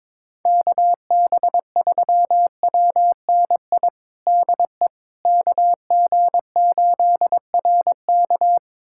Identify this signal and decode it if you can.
Morse code, heard as a keyed tone, about 21 wpm, 700 Hz. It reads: KB3WNI DE KG8RK
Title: USA Callsign DE Callsign 22wpm